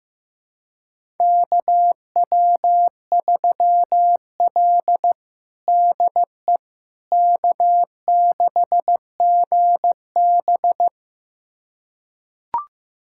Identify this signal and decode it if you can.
Morse code, heard as a keyed tone, about 15 wpm, 700 Hz. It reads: KW3L DE K6GB
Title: USA Callsign DE Callsign - 4 Characters - 15wpm